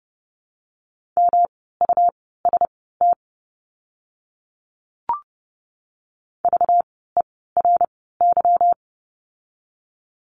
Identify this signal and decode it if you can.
Morse code, heard as a keyed tone, about 30 wpm, spaced as for 17 wpm, 700 Hz. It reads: MUST VERY
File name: course / 30:10wpm / 237 Lesson 38f - Letter q - Words 30wpm